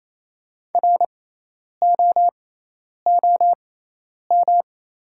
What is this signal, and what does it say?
Morse code, heard as a keyed tone, about 28 wpm, spaced as for 10 wpm, 700 Hz. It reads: ROOM
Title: 6x Character Spacing - Top 300 Words - 28wpm